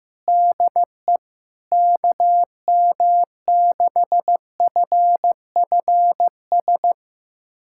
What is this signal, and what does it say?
Morse code, heard as a keyed tone, about 15 wpm, 700 Hz. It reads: DE KM6FFS
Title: USA Callsign DE Callsign 15wpm